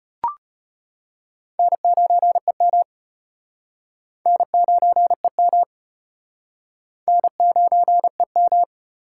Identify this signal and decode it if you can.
Morse code, heard as a keyed tone, about 33 wpm, 700 Hz. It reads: N9EM N9EM N9EM